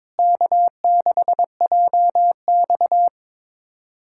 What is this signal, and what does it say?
Morse code, heard as a keyed tone, about 21 wpm, 700 Hz. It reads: K6JX
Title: USA Callsign DE Callsign 22wpm